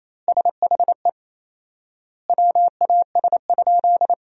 Morse code, heard as a keyed tone, about 28 wpm, 700 Hz. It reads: SHE WAS?
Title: 4x Word-Spacing - Sentences from Top 200 Words - 28wpm